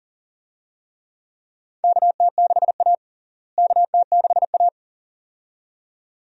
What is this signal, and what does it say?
Morse code, heard as a keyed tone, about 40 wpm, 700 Hz. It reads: KT6A KT6A
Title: USA Call Signs - 4 Characters - Repeated 2x - 40wpm